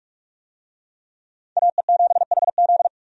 Morse code, heard as a keyed tone, about 47 wpm, 700 Hz. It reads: AE7HZ
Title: USA Call Signs - 5 Characters - 45wpm